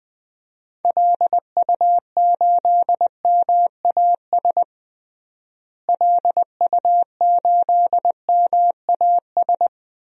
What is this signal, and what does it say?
Morse code, heard as a keyed tone, about 20 wpm, 700 Hz. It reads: LU8MAS LU8MAS
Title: DXCC-100 Call Signs - 6 Characters - Repeated 2x - 20wpm